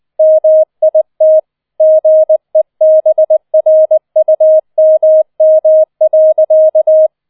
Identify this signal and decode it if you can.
Morse code, heard as a keyed tone, about 20 wpm, 630 Hz. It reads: MIT GEBRUMM.